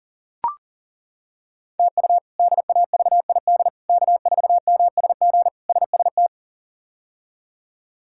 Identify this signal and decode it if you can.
Morse code, heard as a keyed tone, about 40 wpm, 690 Hz. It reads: TU DAVID K4MSG SST
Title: K1USN SST QSO Practice - 40wpm